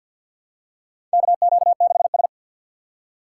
Morse code, heard as a keyed tone, about 50 wpm, 700 Hz. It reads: KQ6S
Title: USA Call Signs - 4 Characters - 2x Character Spacing - 50wpm